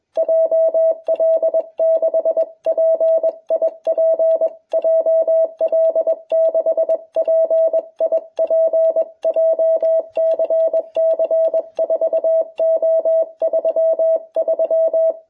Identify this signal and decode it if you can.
Morse code, heard as a keyed tone, about 22 wpm, 660 Hz. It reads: JL6PIP JL6PIP JCC 4O33